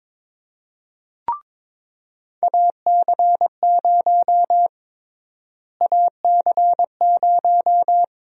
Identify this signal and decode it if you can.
Morse code, heard as a keyed tone, about 22 wpm, 700 Hz. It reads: AC0 AC0
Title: USA Call Sign Prefixes - Repeated 2x - 22wpm